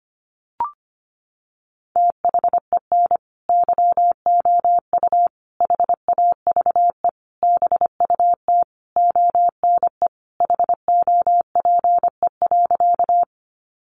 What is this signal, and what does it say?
Morse code, heard as a keyed tone, about 25 wpm, 700 Hz. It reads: THEN YOU HAVE BUT ONE HOPE.
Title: Sentences from Top 1000 Words 25wpm